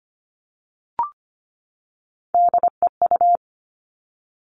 Morse code, heard as a keyed tone, about 25 wpm, 700 Hz. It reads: DEU
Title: Three Challenging Letter-Number Combinations 25wpm